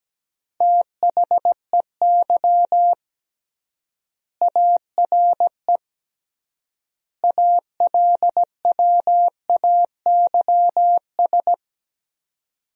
Morse code, heard as a keyed tone, about 17 wpm, 700 Hz. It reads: THEY ARE ALWAYS
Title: 3x Word-Spacing - Sentences from Top 300 Words - 17wpm